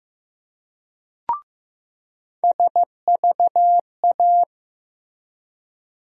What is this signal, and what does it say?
Morse code, heard as a keyed tone, about 15 wpm, 700 Hz. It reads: SVA
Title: USA Call Sign Postfixes - 3 Letters 15wpm